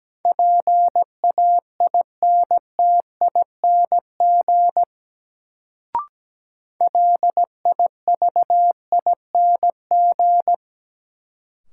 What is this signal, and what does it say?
Morse code, heard as a keyed tone, about 17 wpm, 700 Hz. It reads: PAINTING LIVING
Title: Sentences from Top 1000 Words - Review Part 6 - New Word Drill - 17wpm